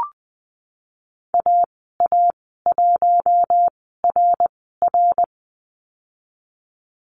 Morse code, heard as a keyed tone, about 20 wpm, spaced as for 15 wpm, 700 Hz. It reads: AA1RR
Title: 167 Lesson 27g - Number 7 - Call Signs 20wpm